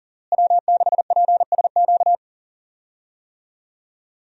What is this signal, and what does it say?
Morse code, heard as a keyed tone, about 40 wpm, 700 Hz. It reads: W6PSQ
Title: USA Call Signs - 5 Characters - Repeated 3x - 40wpm